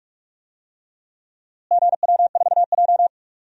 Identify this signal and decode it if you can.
Morse code, heard as a keyed tone, about 41 wpm, 700 Hz. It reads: GW4J
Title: DXCC-100 Call Signs - 4 Characters - Repeated 3x - 45wpm